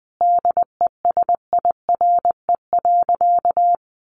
Morse code, heard as a keyed tone, about 20 wpm, 700 Hz. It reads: DESIRE.
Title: Sentences from Top 1300 Words 20wpm